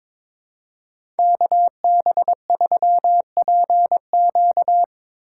Morse code, heard as a keyed tone, about 22 wpm, 700 Hz. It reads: KB3PQ
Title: USA Call Signs - 5 Characters - 4x Character Spacing - 22wpm